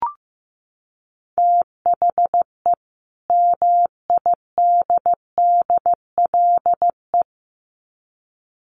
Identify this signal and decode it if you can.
Morse code, heard as a keyed tone, about 15 wpm, 700 Hz. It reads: THE MIDDLE
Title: Sets of 2 Words 15wpm